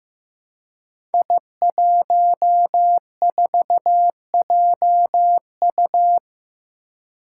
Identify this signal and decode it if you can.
Morse code, heard as a keyed tone, about 15 wpm, 700 Hz. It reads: I14JU
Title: Five Challenging Letter-Number Combinations 15wpm